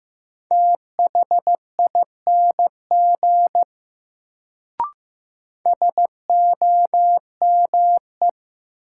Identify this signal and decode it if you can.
Morse code, heard as a keyed tone, about 15 wpm, 700 Hz. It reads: THING SOME
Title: Sentences from Top 100 Words - New Word Drill - 15wpm